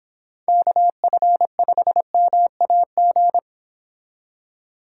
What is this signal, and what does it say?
Morse code, heard as a keyed tone, about 26 wpm, 700 Hz. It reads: KF5MAG